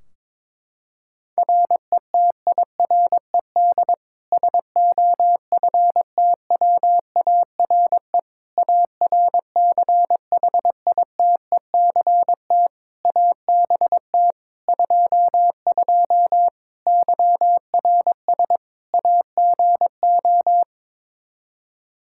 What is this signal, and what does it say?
Morse code, heard as a keyed tone, about 22 wpm, 700 Hz. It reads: RETIRED SOFTWARE ARCHITECT ABT 22 YRS AGO
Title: Practice QSO - Exchange 3 - More Info - 22wpm